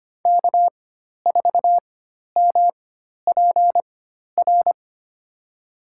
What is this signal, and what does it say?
Morse code, heard as a keyed tone, about 25 wpm, spaced as for 12 wpm, 700 Hz. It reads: K4MPR
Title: USA Call Signs - 5 Characters - 4x Character Spacing - 25wpm